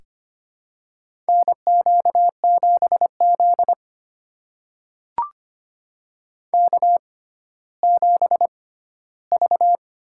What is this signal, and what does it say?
Morse code, heard as a keyed tone, about 25 wpm, spaced as for 14 wpm, 700 Hz. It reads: NQ7Z K7V
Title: USA Call Signs - 4 Characters - 6x Character Spacing - 25wpm